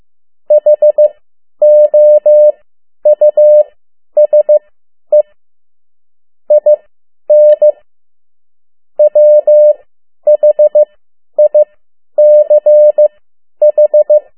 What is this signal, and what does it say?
Morse code, heard as a keyed tone, about 15 wpm, spaced as for 10 wpm, 600 Hz. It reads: HOUSE IN WHICH
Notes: Around the World in 80 Days paragraph one Morse code 10 WPM